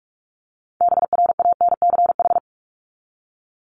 Morse code, heard as a keyed tone, about 44 wpm, 700 Hz. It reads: BRANCH
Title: Top 1751-2000 Words 45wpm